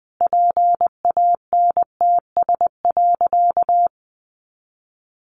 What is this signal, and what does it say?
Morse code, heard as a keyed tone, about 20 wpm, 700 Hz. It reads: PANTS.
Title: A Sound Of Thunder 20wpm